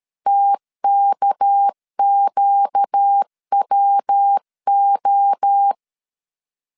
Morse code, heard as a keyed tone, about 13 wpm, 800 Hz. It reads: TKQWO
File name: LEZIONI / 8 YJ45 / 8 YJ4 12wpm 2151